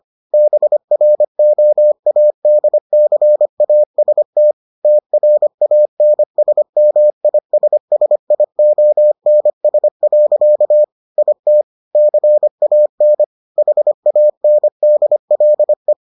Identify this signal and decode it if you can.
Morse code, heard as a keyed tone, about 25 wpm, 600 Hz. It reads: BROADCAST TRANSMISSIONS. IT CAN HANDLE